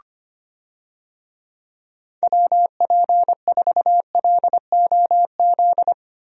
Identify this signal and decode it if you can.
Morse code, heard as a keyed tone, about 25 wpm, 700 Hz. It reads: WP4LOZ
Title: USA Call Signs - 6 Characters - 25wpm